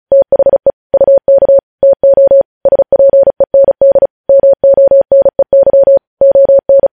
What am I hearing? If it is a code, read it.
Morse code, heard as a keyed tone, about 36 wpm, 550 Hz. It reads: THE UK TO SPEND MONEY ON
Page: news headlines in morse code at 35 wpm